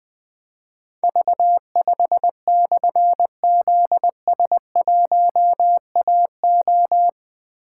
Morse code, heard as a keyed tone, about 20 wpm, 700 Hz. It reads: V5/ZS1AO
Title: Difficult Call Signs 20wpm